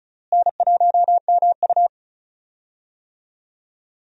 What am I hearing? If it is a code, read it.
Morse code, heard as a keyed tone, about 36 wpm, 700 Hz. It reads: N1MU
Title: USA Call Signs - 4 Characters - Repeated 2x - 35wpm